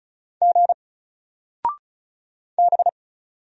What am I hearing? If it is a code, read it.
Morse code, heard as a keyed tone, about 33 wpm, 700 Hz. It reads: G B
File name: Single-Letter-Number-35wpm